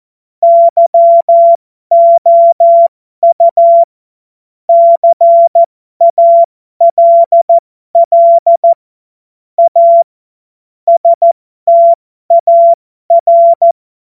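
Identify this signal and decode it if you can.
Morse code, heard as a keyed tone, about 14 wpm, spaced as for 12 wpm, 680 Hz. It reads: YOU CALL A STAR
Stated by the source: Morse joke practice at effective 60, Farnsworth 70.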